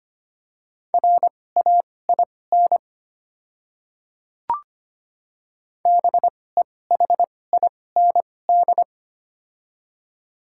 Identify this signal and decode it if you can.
Morse code, heard as a keyed tone, about 25 wpm, spaced as for 18 wpm, 700 Hz. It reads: RAIN BEHIND
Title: 2x Character Spacing - Top 500 Words - 25wpm